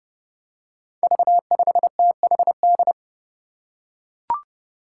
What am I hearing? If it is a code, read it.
Morse code, heard as a keyed tone, about 30 wpm, 700 Hz. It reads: V5THD
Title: Five Challenging Letter-Number Combinations 30wpm